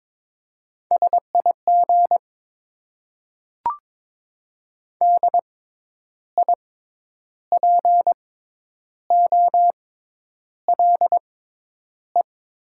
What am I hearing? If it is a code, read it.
Morse code, heard as a keyed tone, about 22 wpm, spaced as for 9 wpm, 700 Hz. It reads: SIG DIPOLE
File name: Top 90 QSO Elements - 6x Character Spacing - 22wpm